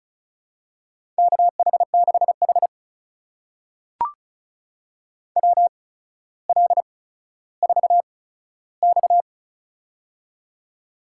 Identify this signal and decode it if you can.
Morse code, heard as a keyed tone, about 36 wpm, spaced as for 15 wpm, 700 Hz. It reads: KH6H WL4X
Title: USA Call Signs - 8x Character Spacing - 35wpm